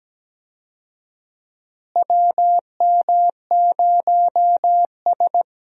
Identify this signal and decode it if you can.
Morse code, heard as a keyed tone, about 17 wpm, 700 Hz. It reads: WM0S
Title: USA Call Signs - 4 Characters - Repeated 3x - 17wpm